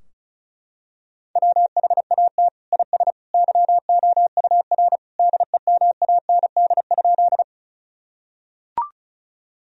Morse code, heard as a keyed tone, about 35 wpm, 700 Hz. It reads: WHAT IS YOUR DEMAND?
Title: Sentences from Top 1000 Words - Review Part 4 - 35wpm